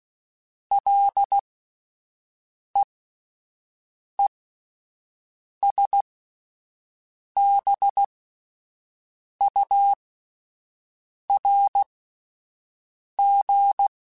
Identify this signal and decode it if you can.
Morse code, heard as a keyed tone, about 16 wpm, spaced as for 5 wpm, 800 Hz. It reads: LEESBURG